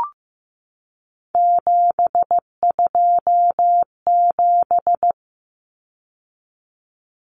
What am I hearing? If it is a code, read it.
Morse code, heard as a keyed tone, about 15 wpm, 700 Hz. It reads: 727
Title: Numbers - 3-Digit 15wpm